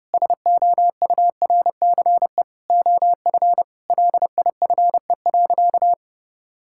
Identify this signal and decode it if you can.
Morse code, heard as a keyed tone, about 30 wpm, 700 Hz. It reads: SOURCE OF LIFE.
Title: Sentences from Top 1000 Words - Review Part 5 - 30wpm